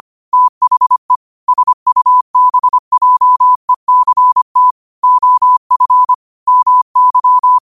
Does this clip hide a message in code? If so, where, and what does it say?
Yes: Morse code from 0.3 to 7.6 s, heard as a keyed tone, about 25 wpm, 1000 Hz. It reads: THE SUBJECT OF MY